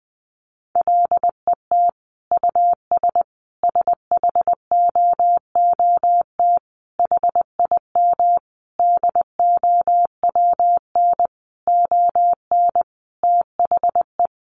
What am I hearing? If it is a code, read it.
Morse code, heard as a keyed tone, about 20 wpm, 700 Hz. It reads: LET US SHOOT HIM DOWN ON THE
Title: Sentences from Top 1200 Words 20wpm